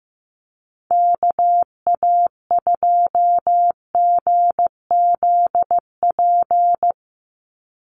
Morse code, heard as a keyed tone, about 14 wpm, 700 Hz. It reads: KA2GZP
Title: USA Call Signs 15wpm